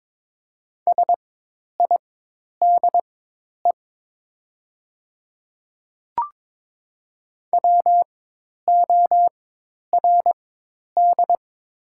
Morse code, heard as a keyed tone, about 22 wpm, spaced as for 10 wpm, 700 Hz. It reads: SIDE WORD
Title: 4x Character Spacing - Top 100 Words - 22wpm